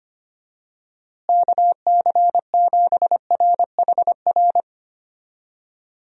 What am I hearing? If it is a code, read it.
Morse code, heard as a keyed tone, about 25 wpm, 700 Hz. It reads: KC7RHR
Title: USA Call Signs - 6 Characters - Repeated 3x - 25wpm